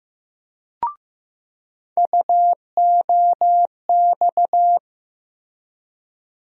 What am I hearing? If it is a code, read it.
Morse code, heard as a keyed tone, about 15 wpm, 700 Hz. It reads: UOX
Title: USA Call Sign Postfixes - 3 Letters 15wpm